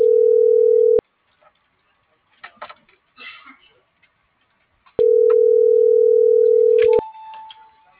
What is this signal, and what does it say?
Ringback tone (North America), 440 + 480 Hz.